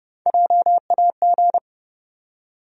Morse code, heard as a keyed tone, about 30 wpm, 700 Hz. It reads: JAG
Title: USA Call Sign Postfixes - 3 Letters 30wpm